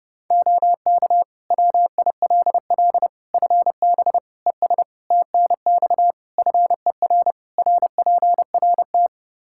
Morse code, heard as a keyed tone, about 32 wpm, 700 Hz. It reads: OK WILL FB ES TNX FER RPRT
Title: Practice QSO - Exchange 2 - Info - 30wpm